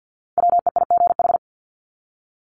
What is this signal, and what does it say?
Morse code, heard as a keyed tone, about 50 wpm, 700 Hz. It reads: WEIGH
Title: Top 1751-2000 Words 50wpm